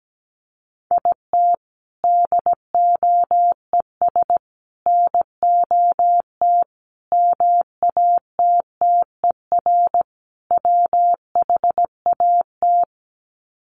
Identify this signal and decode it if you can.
Morse code, heard as a keyed tone, about 17 wpm, 700 Hz. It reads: IT DOES NOT MATTER WHAT
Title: Sets of 5 Words Encore 17wpm